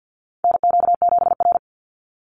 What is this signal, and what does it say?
Morse code, heard as a keyed tone, about 49 wpm, 700 Hz. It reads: NQ7R
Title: NA SOTA Chaser Callsigns 50wpm